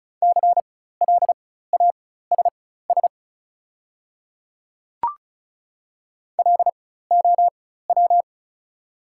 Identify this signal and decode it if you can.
Morse code, heard as a keyed tone, about 34 wpm, spaced as for 16 wpm, 690 Hz. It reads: CLASS LOW
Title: 4x Character Spacing - Top 500 Words - 35wpm